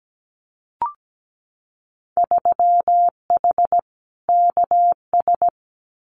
Morse code, heard as a keyed tone, about 17 wpm, 700 Hz. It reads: 3H KS
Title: Winter Field Day Exchanges 17wpm